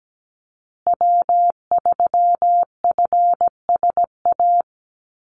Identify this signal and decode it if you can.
Morse code, heard as a keyed tone, about 17 wpm, 700 Hz. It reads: W3FSA
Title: Call Signs Repeated 2x - 17wpm